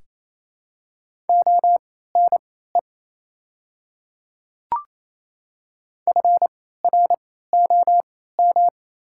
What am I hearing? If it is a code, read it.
Morse code, heard as a keyed tone, about 28 wpm, spaced as for 16 wpm, 700 Hz. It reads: ONE FROM
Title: 3x Character Spacing - Top 100 Words - 28wpm